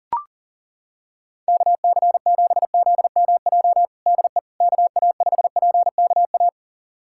Morse code, heard as a keyed tone, about 40 wpm, 700 Hz. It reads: KC7ZMJ DE KA5PKA
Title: USA Callsign DE Callsign - 6 Characters - 40wpm